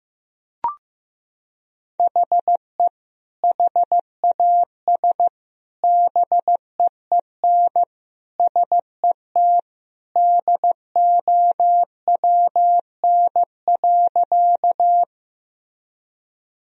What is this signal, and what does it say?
Morse code, heard as a keyed tone, about 15 wpm, 700 Hz. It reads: HE HAS BEEN SET DOWN.
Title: Sentences from Top 1000 Words - Review Part 1 - 15wpm